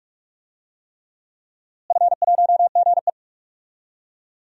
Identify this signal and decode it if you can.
Morse code, heard as a keyed tone, about 47 wpm, 700 Hz. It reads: F1GE
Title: DXCC-100 Call Signs - 4 Characters - Repeated 3x - 45wpm